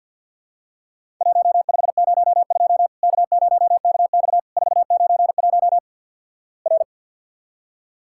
Morse code, heard as a keyed tone, about 49 wpm, 700 Hz. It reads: 1502 K0KX 491 R
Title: Worked All Europe DX Contest - QTC Exchange with Repeats - 50wpm